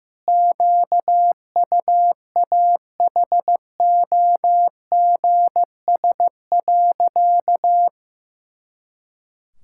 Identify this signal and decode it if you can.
Morse code, heard as a keyed tone, about 15 wpm, 700 Hz. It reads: QUAHOGS.
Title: Pangram Challenge v2 - 15wpm